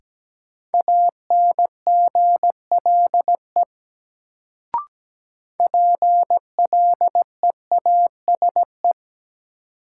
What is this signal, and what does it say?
Morse code, heard as a keyed tone, about 17 wpm, 700 Hz. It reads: ANGLE PLEASE
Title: Sentences from Top 1500 Words - New Word Drill Alt - 17wpm